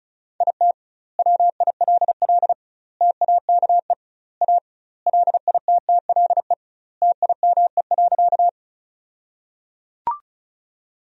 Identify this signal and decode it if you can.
Morse code, heard as a keyed tone, about 36 wpm, 700 Hz. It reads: IT WILL TAKE A LITTLE TIME.
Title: 2x Word-Spacing - Sentences from Top 200 Words - 35wpm